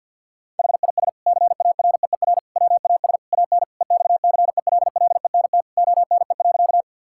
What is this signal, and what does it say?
Morse code, heard as a keyed tone, about 53 wpm, 700 Hz. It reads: HIS CAREER WAS AN EXCELLENT ONE.
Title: Sentences from Top 1000 Words - Review Part 6 - 50wpm